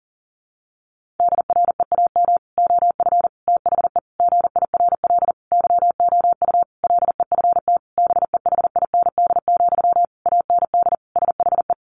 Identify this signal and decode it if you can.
Morse code, heard as a keyed tone, about 40 wpm, 700 Hz. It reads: DREAM OF THE GIRL YOU LEFT BEHIND, AND SHE
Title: Sentences from Top 1500 Words 40wpm